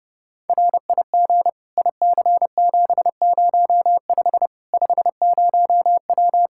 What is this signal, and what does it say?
Morse code, heard as a keyed tone, about 30 wpm, 700 Hz. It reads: RIG IC705 50W
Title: Practice QSO - Exchange 2 - Info - 30wpm